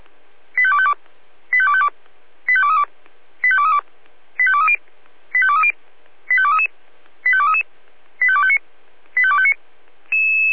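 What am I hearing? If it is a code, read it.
Five-tone selective call (ZVEI), heard as quick runs of five tones, tones of about 70 ms. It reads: 85252 85252 85232 85232 85230 85230 85233 85233 85259 85259
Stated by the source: ZVEI five-tone selective call 85885 received on an SDR.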